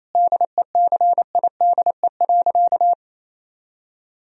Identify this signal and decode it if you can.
Morse code, heard as a keyed tone, about 27 wpm, 700 Hz. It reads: DECIDE.
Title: Sentences from Top 1000 Words - Review Part 3 - 28wpm